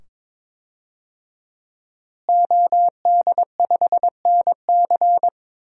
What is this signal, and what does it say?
Morse code, heard as a keyed tone, about 22 wpm, 700 Hz. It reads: OD5NC
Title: DXCC-100 Call Signs - 5 Characters - Repeated 2x - 22wpm